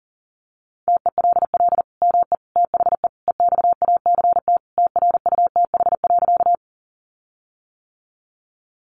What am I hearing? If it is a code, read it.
Morse code, heard as a keyed tone, about 40 wpm, 700 Hz. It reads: TELL ME THE EXACT TRUTH.
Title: Sentences from Top 1300 Words 40wpm